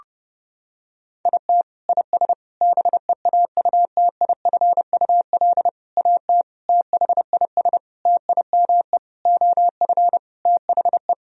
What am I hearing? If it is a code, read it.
Morse code, heard as a keyed tone, about 32 wpm, 700 Hz. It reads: IT IS BEAUTIFUL AT THIS TIME OF THE
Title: Sentences from Top 1000 Words - Review Part 1 - 30wpm